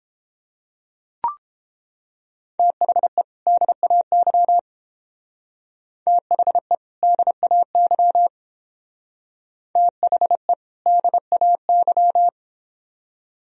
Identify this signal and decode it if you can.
Morse code, heard as a keyed tone, about 31 wpm, 700 Hz. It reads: THE DAY THE DAY THE DAY